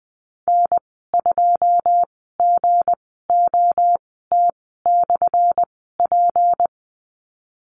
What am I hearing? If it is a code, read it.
Morse code, heard as a keyed tone, about 19 wpm, spaced as for 14 wpm, 700 Hz. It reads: N2GOT/P
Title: 179 Lesson 29g - Slash - Call Signs 20wpm